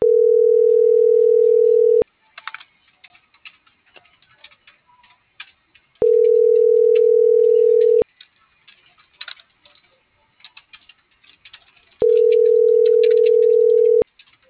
Ringback tone (North America), 440 + 480 Hz.